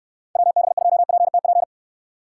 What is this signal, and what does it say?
Morse code, heard as a keyed tone, about 69 wpm, 700 Hz. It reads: WB2PTP